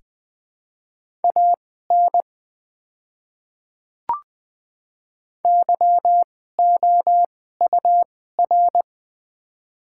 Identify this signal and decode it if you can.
Morse code, heard as a keyed tone, about 20 wpm, spaced as for 14 wpm, 700 Hz. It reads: AN YOUR
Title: 2x Character Spacing - Top 300 Words - 20wpm